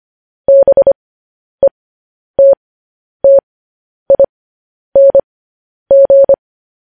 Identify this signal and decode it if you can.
Morse code, heard as a keyed tone, about 25 wpm, spaced as for 10 wpm, 550 Hz. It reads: BETTING
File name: podcast / News headlines-1x-25x10